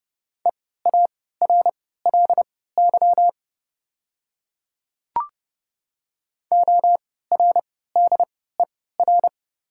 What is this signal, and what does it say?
Morse code, heard as a keyed tone, about 30 wpm, spaced as for 17 wpm, 700 Hz. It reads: EARLY ORDER
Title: 3x Character Spacing - Top 500 Words - 30wpm